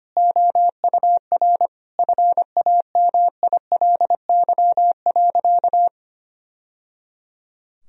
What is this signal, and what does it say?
Morse code, heard as a keyed tone, about 25 wpm, 700 Hz. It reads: OUR FAMILY.
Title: Sentences from Top 1000 Words - Review Part 4 - 25wpm